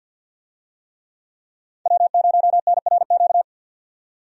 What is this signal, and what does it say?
Morse code, heard as a keyed tone, about 50 wpm, 700 Hz. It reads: W0NRQ
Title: USA Call Signs - 5 Characters - 50wpm